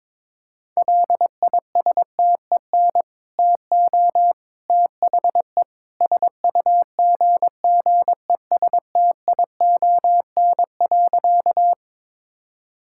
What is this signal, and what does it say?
Morse code, heard as a keyed tone, about 22 wpm, 700 Hz. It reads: LISTEN TO THE SUGGESTION.
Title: Sentences from Top 1600 Words 22wpm